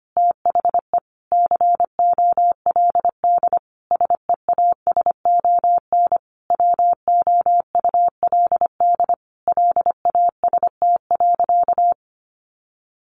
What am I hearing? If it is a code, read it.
Morse code, heard as a keyed tone, about 25 wpm, 700 Hz. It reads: THE COLD SEASON WOULD LAST.
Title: Sentences from Top 1400 Words 25wpm